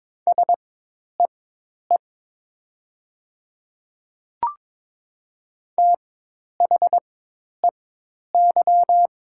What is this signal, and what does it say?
Morse code, heard as a keyed tone, about 22 wpm, spaced as for 10 wpm, 690 Hz. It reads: SEE THEY
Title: 4x Character Spacing - Top 100 Words - 22wpm